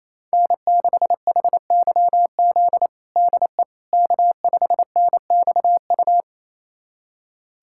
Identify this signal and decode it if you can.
Morse code, heard as a keyed tone, about 28 wpm, 700 Hz. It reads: N6HYZ DE K5NXU
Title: USA Callsign DE Callsign - 5 Characters - 28wpm